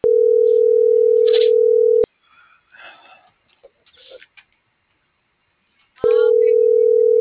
Ringback tone (North America), 440 + 480 Hz.
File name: CENTRAL MABE